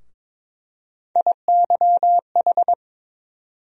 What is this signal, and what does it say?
Morse code, heard as a keyed tone, about 22 wpm, 700 Hz. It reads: IYH
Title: USA Call Sign Postfixes - 3 Letters 22wpm